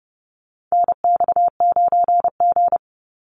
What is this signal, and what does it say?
Morse code, heard as a keyed tone, about 30 wpm, 700 Hz. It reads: NX9G
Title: Call Signs - 30wpm